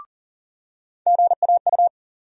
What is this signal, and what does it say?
Morse code, heard as a keyed tone, about 40 wpm, 700 Hz. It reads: GAU
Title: USA Call Sign Postfixes - 3 Letters 40wpm